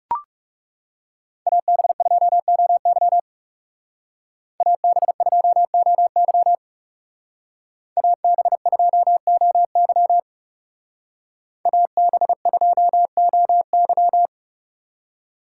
Morse code, heard as a keyed tone, about 39 wpm, 700 Hz. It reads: AB2OY AB2OY AB2OY AB2OY